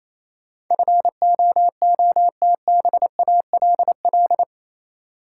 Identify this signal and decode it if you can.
Morse code, heard as a keyed tone, about 28 wpm, 700 Hz. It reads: FOOTBALL
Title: Sentences from Top 1600 Words - New Word Drill Alt - 28wpm